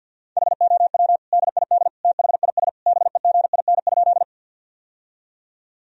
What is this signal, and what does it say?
Morse code, heard as a keyed tone, about 47 wpm, 700 Hz. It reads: HOW DID THIS BEGIN?
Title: Sentences from Top 1000 Words - Review Part 1 - 50wpm